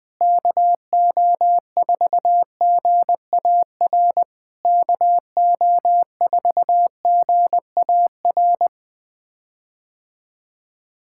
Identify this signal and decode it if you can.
Morse code, heard as a keyed tone, about 20 wpm, 700 Hz. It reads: KO4GAR KO4GAR
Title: POTA QSO Practice - 20wpm